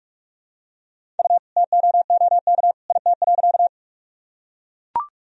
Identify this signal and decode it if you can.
Morse code, heard as a keyed tone, about 44 wpm, 700 Hz. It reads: U TOOK IT.
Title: Sentences from Top 300 Words and Abbreviations 45wpm